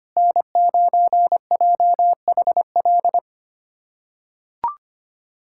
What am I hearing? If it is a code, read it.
Morse code, heard as a keyed tone, about 25 wpm, 700 Hz. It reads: N9JHL
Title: USA Callsign DE Callsign - 5 Characters - 25wpm